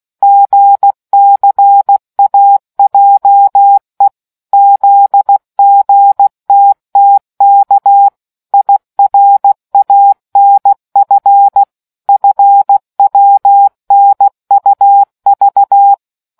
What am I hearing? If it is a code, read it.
Morse code, heard as a keyed tone, about 16 wpm, 800 Hz. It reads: GCAJE ZGTTK IRANF FWNUV